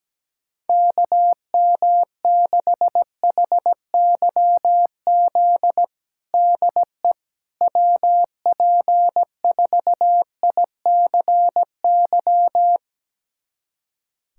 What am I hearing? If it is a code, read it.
Morse code, heard as a keyed tone, about 17 wpm, 700 Hz. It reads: KM6HYZ DE WP4ICY
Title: USA Callsign DE Callsign - 6 Characters - 17wpm